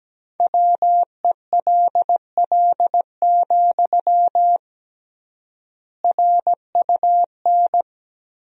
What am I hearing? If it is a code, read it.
Morse code, heard as a keyed tone, about 17 wpm, 700 Hz. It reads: WELL, RUN
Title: 3x Word-Spacing - Sentences from Top 300 Words - 17wpm